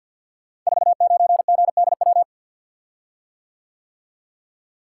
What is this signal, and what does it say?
Morse code, heard as a keyed tone, about 52 wpm, 700 Hz. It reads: 49GDW
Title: Code Groups - 5 Letter:Numbers - 50wpm